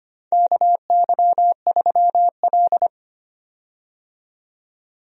Morse code, heard as a keyed tone, about 25 wpm, 700 Hz. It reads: KY3L
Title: USA Call Signs - 4 Characters - 25wpm